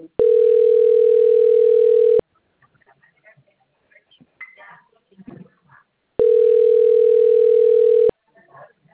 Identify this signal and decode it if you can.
Ringback tone (North America), 440 + 480 Hz.
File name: out- - - - - .62579